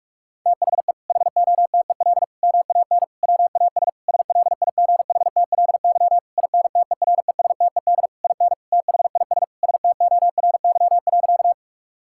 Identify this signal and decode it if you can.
Morse code, heard as a keyed tone, about 46 wpm, 700 Hz. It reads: THE HOTEL MAN WAS SLIGHTLY INTERESTED IN THIS STORY.
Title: Sentences from Top 1600 Words 45wpm